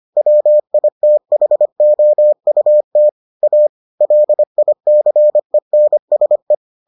Morse code, heard as a keyed tone, about 25 wpm, 600 Hz. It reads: WITHOUT A LICENSE